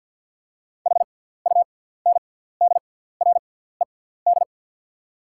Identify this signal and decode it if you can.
Morse code, heard as a keyed tone, about 48 wpm, spaced as for 17 wpm, 700 Hz. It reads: HUNDRED
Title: 6x Character Spacing - Top 500 Words - 50wpm